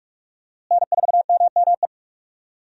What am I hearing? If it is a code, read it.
Morse code, heard as a keyed tone, about 46 wpm, 700 Hz. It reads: N4MME
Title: USA Call Signs - 5 Characters - Repeated 3x - 45wpm